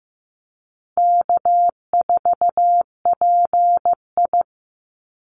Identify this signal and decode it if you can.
Morse code, heard as a keyed tone, about 15 wpm, 700 Hz. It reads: K4PI
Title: Easy Call Signs Repeated 3x - 15wpm